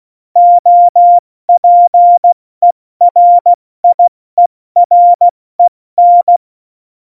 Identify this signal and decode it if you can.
Morse code, heard as a keyed tone, about 16 wpm, spaced as for 14 wpm, 700 Hz. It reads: OPERIEREN
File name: Morsekurs / Witze / Witz 08. September 2020, 70 F80